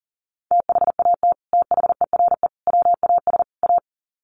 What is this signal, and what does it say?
Morse code, heard as a keyed tone, about 40 wpm, 700 Hz. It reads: THAT THERE WAS A
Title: Sets of 4 Words 40wpm